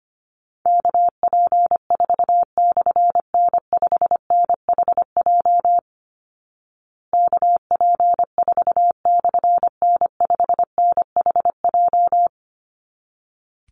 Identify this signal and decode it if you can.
Morse code, heard as a keyed tone, about 25 wpm, 700 Hz. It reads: KP4/N5NHJ KP4/N5NHJ
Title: Call Signs Repeated 3x - 25wpm